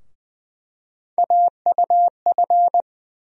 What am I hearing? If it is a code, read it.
Morse code, heard as a keyed tone, about 20 wpm, 700 Hz. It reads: AUF
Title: USA Call Sign Postfixes - 3 Letters 20wpm